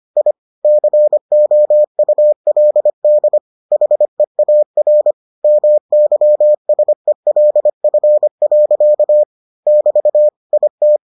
Morse code, heard as a keyed tone, about 25 wpm, 600 Hz. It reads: I COULD HEAR MYSELF. <BT> IT